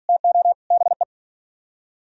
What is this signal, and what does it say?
Morse code, heard as a keyed tone, about 51 wpm, 700 Hz. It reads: TO BE